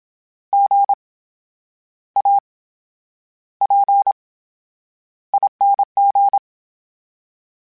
Morse code, heard as a keyed tone, about 27 wpm, spaced as for 9 wpm, 800 Hz. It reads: GAPING
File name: Special INGs / Special INGs-mp3s / VerySlowToFaster Ing @25